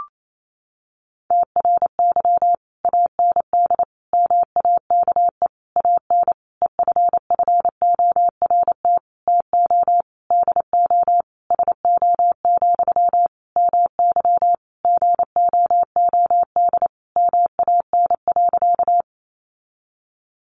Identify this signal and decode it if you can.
Morse code, heard as a keyed tone, about 28 wpm, 700 Hz. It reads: TRY AND MAKE AN EFFORT TO DO SO, MY GOOD MAN.
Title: Sentences from Top 1200 Words 28wpm